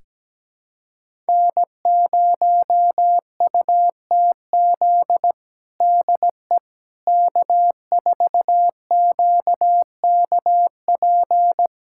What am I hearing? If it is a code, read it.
Morse code, heard as a keyed tone, about 17 wpm, 700 Hz. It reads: N0UTZ DE K4QKP
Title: USA Callsign DE Callsign - 5 Characters - 17wpm